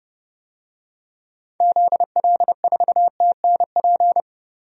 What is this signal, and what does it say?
Morse code, heard as a keyed tone, about 30 wpm, 700 Hz. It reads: ZL4TNP
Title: DXCC-100 Call Signs - 6 Characters - Repeated 3x - 30wpm